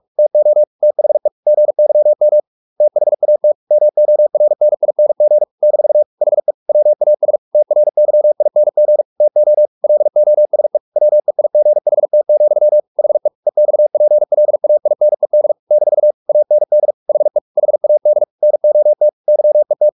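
Morse code, heard as a keyed tone, about 44 wpm, 600 Hz. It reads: TO THE GYM THAT MORNING - HE WAS TRYING TO LOSE WEIGHT, HE EXPLAINED - AND HE HAD NOT YET
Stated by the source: Morse practice at 45 wpm, the novel Never by Ken Follett in English, from CW med Gnister.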